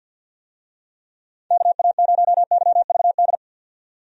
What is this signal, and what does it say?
Morse code, heard as a keyed tone, about 50 wpm, 700 Hz. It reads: KA0YVD
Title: USA Call Signs - 6 Characters - 50wpm